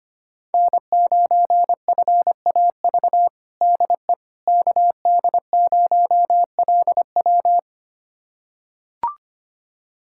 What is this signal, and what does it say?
Morse code, heard as a keyed tone, about 25 wpm, 700 Hz. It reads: N9FAV DE KD0LW
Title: USA Callsign DE Callsign - 5 Characters - 25wpm